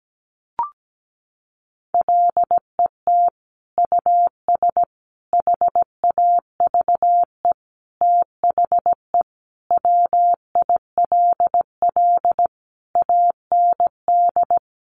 Morse code, heard as a keyed tone, about 17 wpm, 700 Hz. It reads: LET US HAVE THE WILL AND
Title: Sentences from Top 1300 Words 17wpm